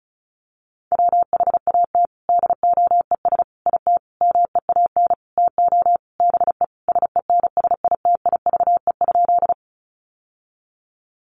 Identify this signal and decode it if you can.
Morse code, heard as a keyed tone, about 34 wpm, 700 Hz. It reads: WHAT DOES IT MEAN TO BE SENSITIVE?
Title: Sentences from Top 1400 Words 35wpm